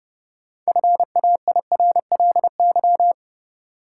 Morse code, heard as a keyed tone, about 30 wpm, 700 Hz. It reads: FAIRLY
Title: Sentences from Top 1100 Words - New Word Drill - 30wpm